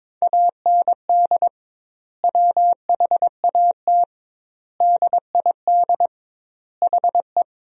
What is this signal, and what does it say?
Morse code, heard as a keyed tone, about 22 wpm, 700 Hz. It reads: AND WHAT DID HE
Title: 2x Word-Spacing - Sentences from Top 200 Words - 22wpm